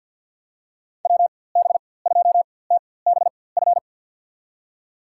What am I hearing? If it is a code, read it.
Morse code, heard as a keyed tone, about 53 wpm, spaced as for 24 wpm, 700 Hz. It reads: WB2TBF
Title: USA Call Signs - 6 Characters - 4x Character Spacing - 50wpm